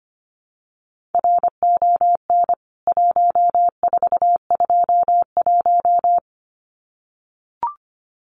Morse code, heard as a keyed tone, about 25 wpm, 700 Hz. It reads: RON 1421
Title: CWT Exchange 25wpm